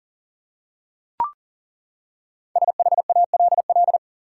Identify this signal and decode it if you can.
Morse code, heard as a keyed tone, about 37 wpm, 700 Hz. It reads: SHALL
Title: Five Letter Words - 40wpm